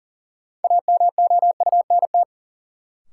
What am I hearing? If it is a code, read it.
Morse code, heard as a keyed tone, about 40 wpm, 700 Hz. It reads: AMOUNT
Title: Sentences from Top 700 Words - New Word Drill - 40wpm